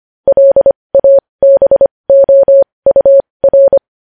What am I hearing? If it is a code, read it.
Morse code, heard as a keyed tone, about 25 wpm, spaced as for 20 wpm, 550 Hz. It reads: LABOUR
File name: podcast / News headlines-1x-25x20